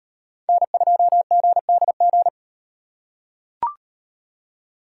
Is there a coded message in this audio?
Yes: Morse code from 0.5 to 2.3 s, heard as a keyed tone, about 38 wpm, 700 Hz. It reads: N2GDG